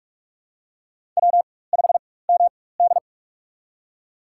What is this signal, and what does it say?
Morse code, heard as a keyed tone, about 41 wpm, spaced as for 20 wpm, 700 Hz. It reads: W5MD
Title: USA Call Signs - 4 Characters - 4x Character Spacing - 45wpm